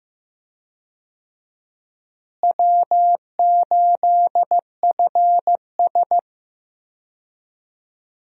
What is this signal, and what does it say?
Morse code, heard as a keyed tone, about 15 wpm, 700 Hz. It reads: W8FS
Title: USA Call Signs - 4 Characters - Repeated 3x - 15wpm